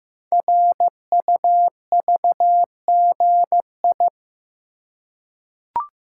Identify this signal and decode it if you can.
Morse code, heard as a keyed tone, about 15 wpm, 700 Hz. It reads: RUVGI
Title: Code Groups - 5 Letter:Numbers - 15wpm